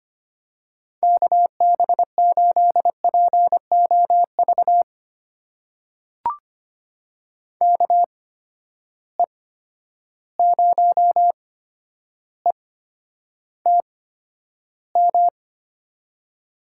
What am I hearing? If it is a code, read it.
Morse code, heard as a keyed tone, about 25 wpm, spaced as for 11 wpm, 700 Hz. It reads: KB8POV KE0ETM
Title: USA Call Signs - 6 Characters - 8x Character Spacing - 25wpm